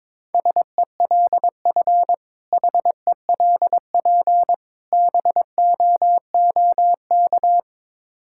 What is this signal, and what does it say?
Morse code, heard as a keyed tone, about 22 wpm, 700 Hz. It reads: SELF HELP BOOK
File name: Difficult Words with Hints - 22wpm